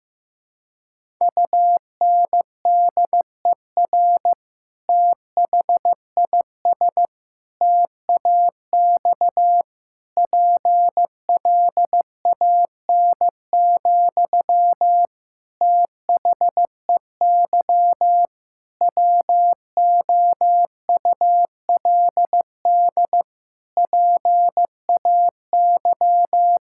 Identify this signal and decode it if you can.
Morse code, heard as a keyed tone, about 15 wpm, 700 Hz. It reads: UNDER THIS TAX PLAN, THEY WOULD PAY
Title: Sentences from Top 1000 Words - Review Part 5 - 15wpm